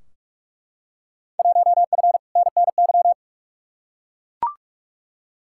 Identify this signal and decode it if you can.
Morse code, heard as a keyed tone, about 42 wpm, 700 Hz. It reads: 1F NNY
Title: Field Day Exchanges - 45wpm